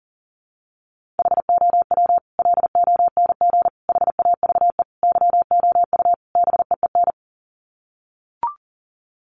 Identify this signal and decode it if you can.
Morse code, heard as a keyed tone, about 40 wpm, 700 Hz. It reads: HOW LONG HAVE YOU BEEN
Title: Sets of 5 Words Encore 40wpm